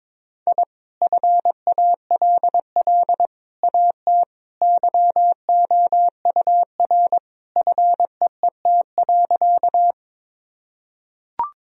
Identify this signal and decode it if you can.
Morse code, heard as a keyed tone, about 22 wpm, 700 Hz. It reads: I FALL AT YOUR FEET.
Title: Sentences from Top 1000 Words - Review Part 2 - 22wpm